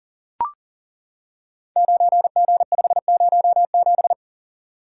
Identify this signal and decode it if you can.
Morse code, heard as a keyed tone, about 42 wpm, 700 Hz. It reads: 9G507
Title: Code Groups - 5 Letter:Numbers - 40wpm